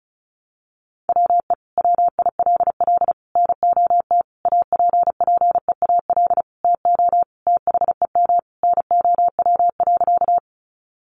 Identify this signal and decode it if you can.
Morse code, heard as a keyed tone, about 34 wpm, 700 Hz. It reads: WE WILL NOT APPEAL TO THEM NOW.
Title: Sentences from Top 1300 Words 35wpm